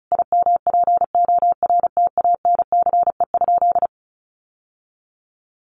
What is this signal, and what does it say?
Morse code, heard as a keyed tone, about 34 wpm, 700 Hz. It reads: IMPORTANCE?
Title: Sentences from Top 1200 Words 35wpm